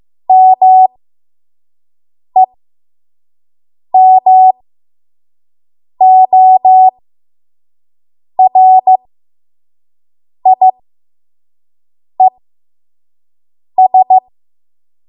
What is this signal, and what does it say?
Morse code, heard as a keyed tone, about 15 wpm, spaced as for 5 wpm, 750 Hz. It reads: MEMORIES
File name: Morse / 05WPM